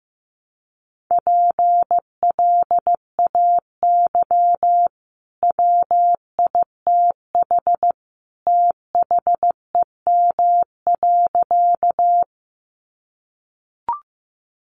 Morse code, heard as a keyed tone, about 15 wpm, 700 Hz. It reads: PLAY WITH THEM.
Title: Sentences from Top 200 Words 15wpm